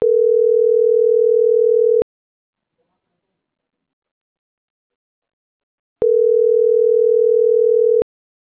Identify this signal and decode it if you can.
Ringback tone (North America), 440 + 480 Hz.